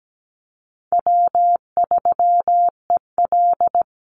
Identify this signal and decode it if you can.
Morse code, heard as a keyed tone, about 17 wpm, 700 Hz. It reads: W3EL
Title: Easy Call Signs - 17wpm